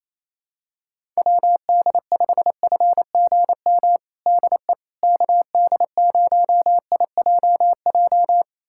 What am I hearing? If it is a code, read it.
Morse code, heard as a keyed tone, about 28 wpm, 700 Hz. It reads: WD5FGM DE KD0IJJ
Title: USA Callsign DE Callsign - 6 Characters - 28wpm